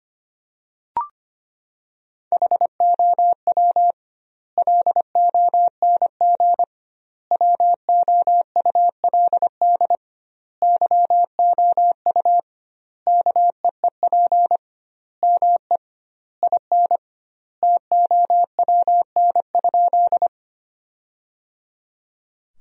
Morse code, heard as a keyed tone, about 25 wpm, 700 Hz. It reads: HOW LONG WOULD YOU KEEP ME IN TOWN?
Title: 2x Word-Spacing - Sentences from Top 500 Words - 25wpm